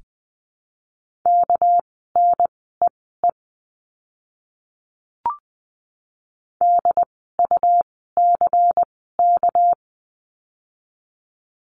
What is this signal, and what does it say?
Morse code, heard as a keyed tone, about 20 wpm, spaced as for 14 wpm, 700 Hz. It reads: KNEE DUCK
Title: 199 Lesson 32f - Letter k - Words 20wpm